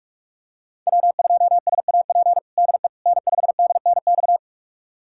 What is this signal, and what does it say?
Morse code, heard as a keyed tone, about 44 wpm, 700 Hz. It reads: W2SAP DE N5DNX
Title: USA Callsign DE Callsign - 5 Characters - 45wpm